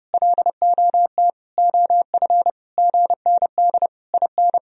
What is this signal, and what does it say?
Morse code, heard as a keyed tone, about 30 wpm, 700 Hz. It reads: LOT OF GND IN
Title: Sentences from Top 500 Words and Abbreviations 30wpm